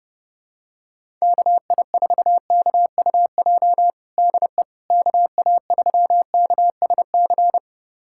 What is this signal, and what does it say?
Morse code, heard as a keyed tone, about 32 wpm, 700 Hz. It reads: KI4KUJ DE KA3KSC
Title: USA Callsign DE Callsign 30wpm